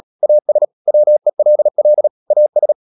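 Morse code, heard as a keyed tone, about 37 wpm, 600 Hz. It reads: AS WELL AS
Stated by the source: Morse code practice at 35–39 wpm sending English text of Ken Follett's novel Never, from CW med Gnister.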